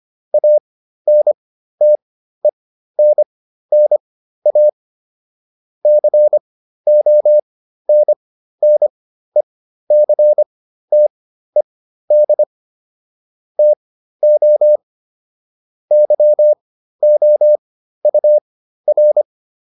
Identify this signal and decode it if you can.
Morse code, heard as a keyed tone, about 25 wpm, spaced as for 13 wpm, 600 Hz. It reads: ANTENNA CONNECTED TO YOUR